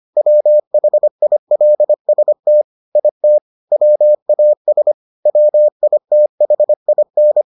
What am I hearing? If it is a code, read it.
Morse code, heard as a keyed tone, about 25 wpm, 600 Hz. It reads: WHILST IT WAS WITHIN